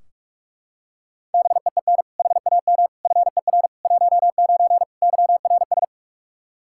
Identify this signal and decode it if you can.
Morse code, heard as a keyed tone, about 44 wpm, 700 Hz. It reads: BEEN HAM FER 19 YRS
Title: Practice QSO - Exchange 3 - More Info - 45wpm